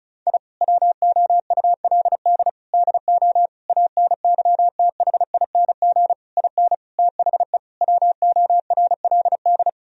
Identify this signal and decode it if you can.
Morse code, heard as a keyed tone, about 36 wpm, 700 Hz. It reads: I WOULD DO ANYTHING IN THE WORLD
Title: Sentences from Top 1000 Words - Review Part 2 - 35wpm